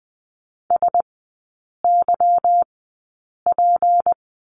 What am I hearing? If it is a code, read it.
Morse code, heard as a keyed tone, about 20 wpm, spaced as for 8 wpm, 700 Hz. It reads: SYP
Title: 152 Lesson 25e - Letter p - Three Characters 20wpm